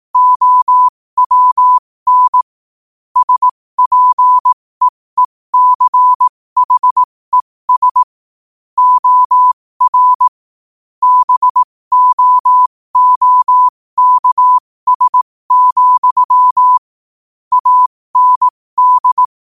Morse code, heard as a keyed tone, about 18 wpm, spaced as for 15 wpm, 1000 Hz. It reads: OWN SPEECHES OR BOOKS, AND
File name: qotdfiles / 15wpm